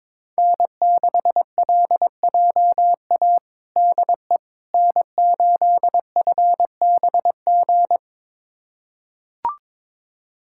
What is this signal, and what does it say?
Morse code, heard as a keyed tone, about 22 wpm, 700 Hz. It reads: N6LJA DE N8FBG
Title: USA Callsign DE Callsign - 5 Characters - 22wpm